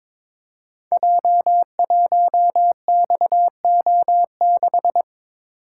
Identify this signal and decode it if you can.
Morse code, heard as a keyed tone, about 22 wpm, 700 Hz. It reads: J1XO6
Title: Code Groups - 5 Letter:Numbers - 22wpm